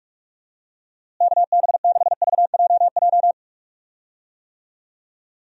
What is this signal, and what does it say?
Morse code, heard as a keyed tone, about 43 wpm, 700 Hz. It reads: KB6VJJ
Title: USA Call Signs - 6 Characters - Repeated 3x - 45wpm